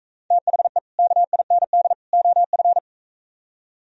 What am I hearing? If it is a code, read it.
Morse code, heard as a keyed tone, about 45 wpm, 700 Hz. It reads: THE KIND OF